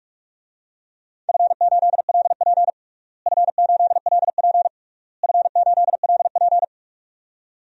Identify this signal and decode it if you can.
Morse code, heard as a keyed tone, about 46 wpm, 700 Hz. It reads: F8LP F8LP F8LP
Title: DXCC-100 Call Signs - 4 Characters - Repeated 3x - 45wpm